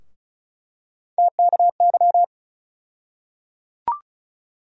Morse code, heard as a keyed tone, about 36 wpm, 700 Hz. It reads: TKY
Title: USA Call Sign Postfixes - 3 Letters 35wpm